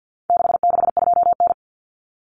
Morse code, heard as a keyed tone, about 50 wpm, 700 Hz. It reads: 662N
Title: Common Cut Numbers - 4-Digit 50wpm